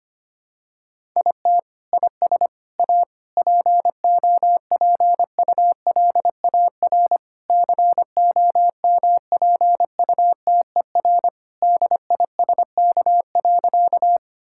Morse code, heard as a keyed tone, about 25 wpm, 700 Hz. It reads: IT IS A POPULAR COMPUTER DISK.
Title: Sentences from Top 1000 Words - Review Part 6 - 25wpm